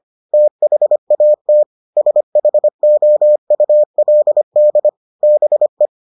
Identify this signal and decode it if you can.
Morse code, heard as a keyed tone, about 25 wpm, 600 Hz. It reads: THAT SHOULD BE